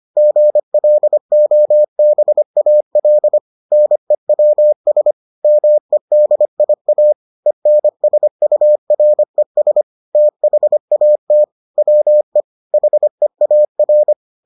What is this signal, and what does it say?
Morse code, heard as a keyed tone, about 25 wpm, 600 Hz. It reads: GLOBAL NEWS MEDIA ENSURES THAT WE HEAR